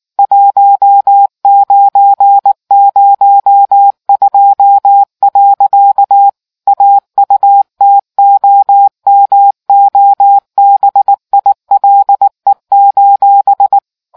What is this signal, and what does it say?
Morse code, heard as a keyed tone, about 19 wpm, 800 Hz. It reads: 1902. AUTOMOBILE: